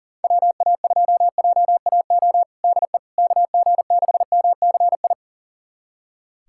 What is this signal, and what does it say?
Morse code, heard as a keyed tone, about 40 wpm, 700 Hz. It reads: WA2JAO DE KG6MCI
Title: USA Callsign DE Callsign 40wpm